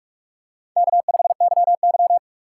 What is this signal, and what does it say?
Morse code, heard as a keyed tone, about 44 wpm, 700 Hz. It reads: K5YY
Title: USA Call Signs - 4 Characters - 2x Character Spacing - 45wpm